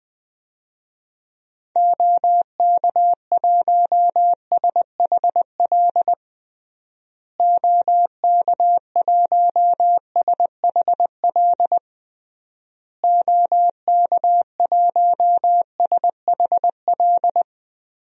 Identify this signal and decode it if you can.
Morse code, heard as a keyed tone, about 20 wpm, 700 Hz. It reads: OK1SHL OK1SHL OK1SHL
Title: DXCC-100 Call Signs - 6 Characters - Repeated 3x - 20wpm